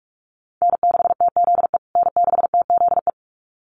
Morse code, heard as a keyed tone, about 44 wpm, 700 Hz. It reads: N6TZE N6TZE
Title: USA Call Signs Repeated 2x 45wpm